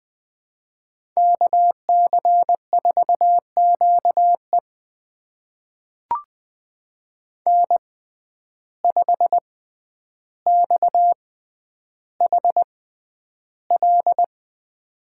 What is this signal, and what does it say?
Morse code, heard as a keyed tone, about 20 wpm, spaced as for 10 wpm, 700 Hz. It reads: KC4QE N5XHL
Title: USA Call Signs - 5 Characters - 6x Character Spacing - 20wpm